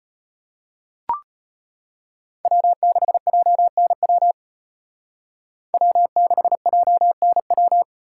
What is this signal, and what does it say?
Morse code, heard as a keyed tone, about 32 wpm, 700 Hz. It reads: W6JNW W6JNW